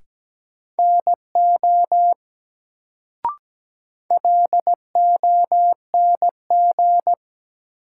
Morse code, heard as a keyed tone, about 17 wpm, 700 Hz. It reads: NO LONG
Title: Sentences from Top 100 Words - New Word Drill - 17wpm